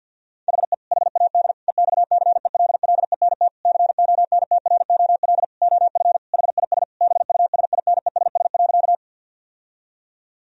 Morse code, heard as a keyed tone, about 50 wpm, 700 Hz. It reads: HE HAD EXCELLENT CONTROL OF HIS BUSINESS.
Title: Sentences from Top 1000 Words - Review Part 3 - 50wpm